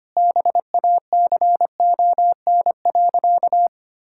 Morse code, heard as a keyed tone, about 25 wpm, 700 Hz. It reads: BACON.
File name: Easy Sentences with Hints - 25wpm